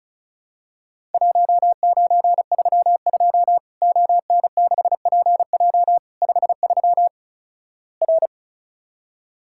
Morse code, heard as a keyed tone, about 36 wpm, 700 Hz. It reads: 1932 ON6PJ 53 R
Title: Worked All Europe DX Contest - QTC Exchange - 35wpm